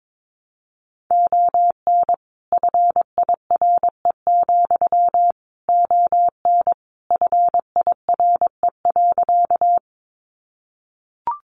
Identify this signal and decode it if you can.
Morse code, heard as a keyed tone, about 22 wpm, 700 Hz. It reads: ON FIRE, ON FIRE.
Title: Sentences from Top 400 Words 22wpm